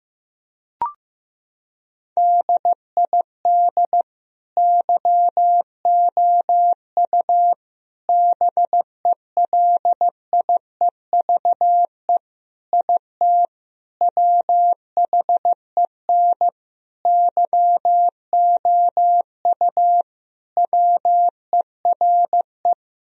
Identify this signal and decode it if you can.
Morse code, heard as a keyed tone, about 15 wpm, 700 Hz. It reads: DID YOU BELIEVE IT WHEN YOU WERE
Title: Sentences from Top 1000 Words - Review Part 1 - 15wpm